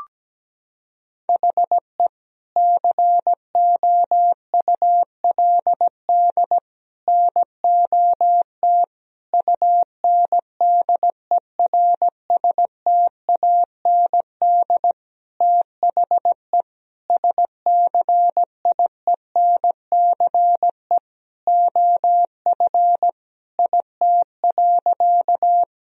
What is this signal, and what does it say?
Morse code, heard as a keyed tone, about 17 wpm, 700 Hz. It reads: HE COULD NOT UNDERSTAND THE SCIENCE OF IT.
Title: Sentences from Top 1000 Words - Review Part 2 - 17wpm